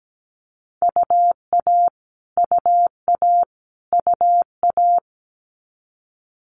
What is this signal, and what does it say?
Morse code, heard as a keyed tone, about 17 wpm, 700 Hz. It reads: UA UA UA
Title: Multi-letter Phonemes Repeated 3x 17wpm